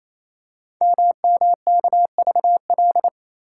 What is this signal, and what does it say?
Morse code, heard as a keyed tone, about 28 wpm, 700 Hz. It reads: MMKVL
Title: Code Groups - 5 Letters - 28wpm